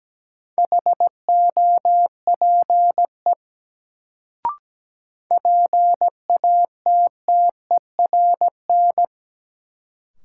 Morse code, heard as a keyed tone, about 17 wpm, 700 Hz. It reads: HOPE PATTERN
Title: Sentences from Top 1000 Words - New Word Drill - 17wpm